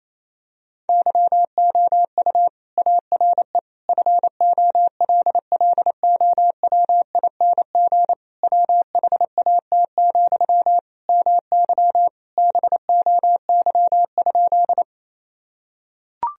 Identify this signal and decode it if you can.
Morse code, heard as a keyed tone, about 28 wpm, 700 Hz. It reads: YOU ARE FOLLOWING WHAT, MY BOY?
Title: Sentences from Top 1000 Words - Review Part 4 - 28wpm